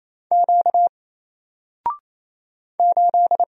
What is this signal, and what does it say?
Morse code, heard as a keyed tone, about 28 wpm, 700 Hz. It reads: Q 8
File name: Single-Letter-Number-28wpm